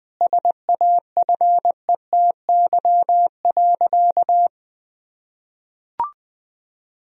Morse code, heard as a keyed tone, about 20 wpm, 700 Hz. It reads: SAFETY.
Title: Sentences from Top 1000 Words - Review Part 4 - 20wpm